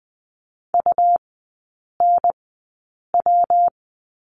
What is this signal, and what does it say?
Morse code, heard as a keyed tone, about 20 wpm, spaced as for 8 wpm, 700 Hz. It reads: UNW
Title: 138 Lesson 23e - Letter f - Three Characters 20wpm